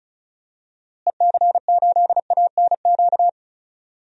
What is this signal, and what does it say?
Morse code, heard as a keyed tone, about 36 wpm, 700 Hz. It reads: EC8ANQ
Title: DXCC-100 Call Signs - 6 Characters - Repeated 2x - 35wpm